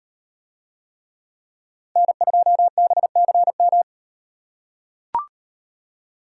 Morse code, heard as a keyed tone, about 36 wpm, 700 Hz. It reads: N2BCM